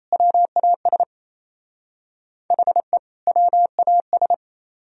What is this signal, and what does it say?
Morse code, heard as a keyed tone, about 34 wpm, 700 Hz. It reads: WAS HE WAS
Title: Speed-Racing - Sets of 2 Words 28wpm